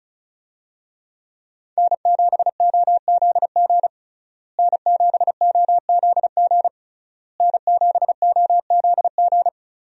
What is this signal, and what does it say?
Morse code, heard as a keyed tone, about 34 wpm, 700 Hz. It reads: N7OZG N7OZG N7OZG
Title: USA Call Signs - 5 Characters - Repeated 3x - 35wpm